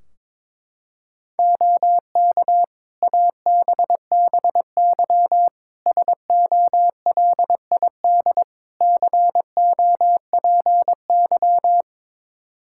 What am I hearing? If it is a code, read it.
Morse code, heard as a keyed tone, about 22 wpm, 700 Hz. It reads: OK ABBY SOLID COPY
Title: Practice QSO - Exchange 3 - More Info - 22wpm